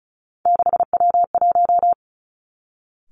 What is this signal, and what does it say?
Morse code, heard as a keyed tone, about 36 wpm, 700 Hz. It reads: 6W1
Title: Three Challenging Letter-Number Combinations 35wpm